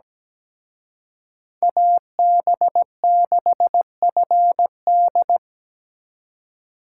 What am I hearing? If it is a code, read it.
Morse code, heard as a keyed tone, about 20 wpm, 700 Hz. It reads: AB6FD
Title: Speed Racing - USA Call Signs 17wpm